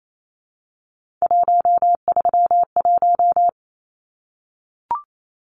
Morse code, heard as a keyed tone, about 28 wpm, 700 Hz. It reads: 131
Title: Numbers - 3-Digit 28wpm